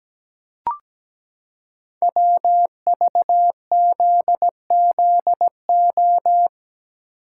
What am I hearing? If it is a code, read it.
Morse code, heard as a keyed tone, about 17 wpm, 700 Hz. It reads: WVZZO
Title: Code Groups - 5 Letters - 17wpm